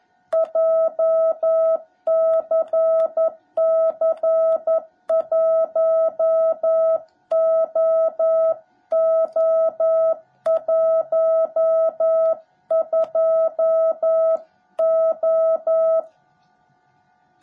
Morse code, heard as a keyed tone, about 11 wpm, 660 Hz. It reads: JCC1OO12O
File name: jh8cbh / 2021cw / cw17b